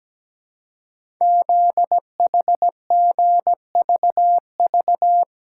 Morse code, heard as a keyed tone, about 17 wpm, 700 Hz. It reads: ZHGVV
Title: Code Groups - 5 Letters - 17wpm